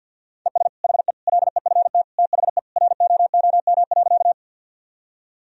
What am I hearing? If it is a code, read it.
Morse code, heard as a keyed tone, about 52 wpm, 700 Hz. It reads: ES HE LEFT THE ROOM.
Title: Sentences from Top 300 Words and Abbreviations 50wpm